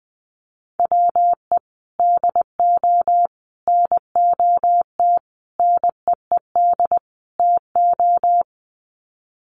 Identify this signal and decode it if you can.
Morse code, heard as a keyed tone, about 20 wpm, 700 Hz. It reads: WE DO NOT NEED TO
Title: Sets of 5 Words Encore 20wpm